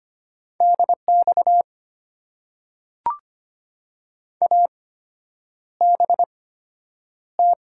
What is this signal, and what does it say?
Morse code, heard as a keyed tone, about 25 wpm, spaced as for 9 wpm, 700 Hz. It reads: DX ABT
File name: Top 90 QSO Elements - 8x Character Spacing - 25wpm